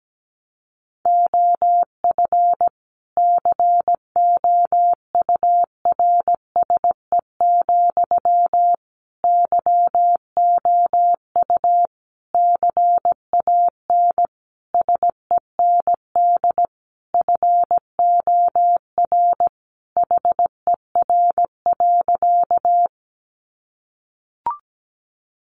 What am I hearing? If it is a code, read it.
Morse code, heard as a keyed tone, about 17 wpm, 700 Hz. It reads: OF COURSE, YOU CAN SEND FOR HER.
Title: Sentences from Top 1500 Words 17wpm